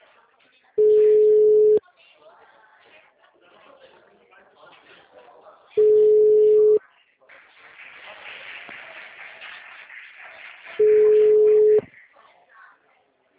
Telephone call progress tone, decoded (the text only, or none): ringback tone